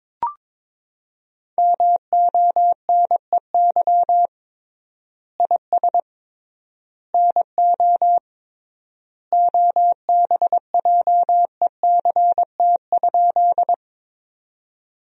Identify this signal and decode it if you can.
Morse code, heard as a keyed tone, about 22 wpm, 700 Hz. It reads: MONEY IS NO OBJECT?
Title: 3x Word-Spacing - Sentences from Top 500 Words - 22wpm